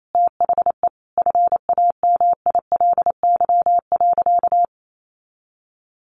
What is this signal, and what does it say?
Morse code, heard as a keyed tone, about 28 wpm, 700 Hz. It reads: THE FAMILY.
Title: Sentences from Top 1100 Words 28wpm